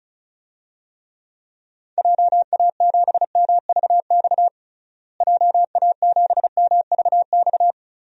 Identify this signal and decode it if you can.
Morse code, heard as a keyed tone, about 34 wpm, 700 Hz. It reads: JA7MVX JA7MVX
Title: DXCC-100 Call Signs - 6 Characters - Repeated 3x - 35wpm